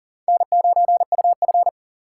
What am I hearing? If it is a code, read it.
Morse code, heard as a keyed tone, about 40 wpm, 700 Hz. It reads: N9UF
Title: USA Call Signs - 4 Characters - 3x Character Spacing - 40wpm